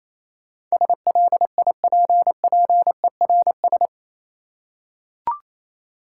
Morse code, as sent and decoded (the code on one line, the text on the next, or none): ... .-.. .. .--. .--. .-. ...
SLIPPERS